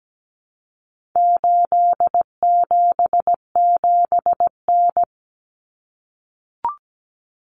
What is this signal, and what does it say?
Morse code, heard as a keyed tone, about 17 wpm, 700 Hz. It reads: 877N
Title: Common Cut Numbers - 4-Digit 17wpm